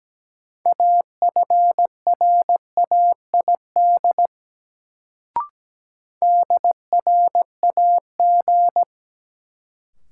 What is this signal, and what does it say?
Morse code, heard as a keyed tone, about 17 wpm, 700 Hz. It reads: AFRAID DRAG
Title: Sentences from Top 1500 Words - New Word Drill Alt - 17wpm